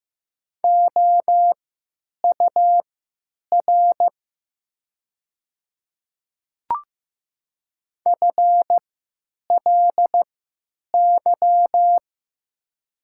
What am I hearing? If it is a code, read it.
Morse code, heard as a keyed tone, about 15 wpm, spaced as for 8 wpm, 700 Hz. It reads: OUR FLY
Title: 3x Character Spacing - Top 500 Words - 15wpm